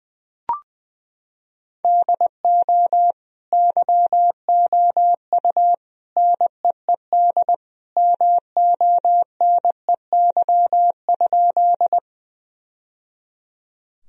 Morse code, heard as a keyed tone, about 21 wpm, 700 Hz. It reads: DO YOU NEED MONEY?
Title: Sentences from Top 1000 Words - Review Part 1 - 20wpm